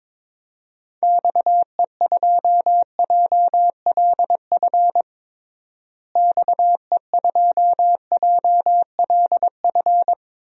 Morse code, heard as a keyed tone, about 22 wpm, 700 Hz. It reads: XE2JLF XE2JLF
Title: DXCC-100 Call Signs - 6 Characters - Repeated 3x - 22wpm